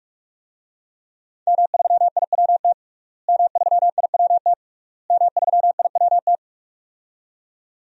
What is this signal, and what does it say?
Morse code, heard as a keyed tone, about 46 wpm, 700 Hz. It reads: M3IWT M3IWT M3IWT
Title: DXCC-100 Call Signs - 5 Characters - Repeated 3x - 45wpm